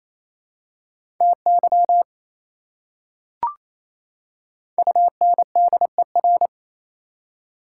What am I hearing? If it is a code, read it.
Morse code, heard as a keyed tone, about 28 wpm, 700 Hz. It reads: TY UNDER
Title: Prefixes and Suffixes 28wpm